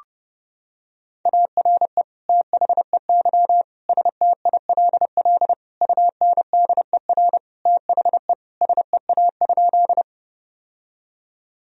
Morse code, heard as a keyed tone, about 32 wpm, 700 Hz. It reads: ARE THEY STILL UNDER THE SEA?
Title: Sentences from Top 1000 Words - Review Part 1 - 30wpm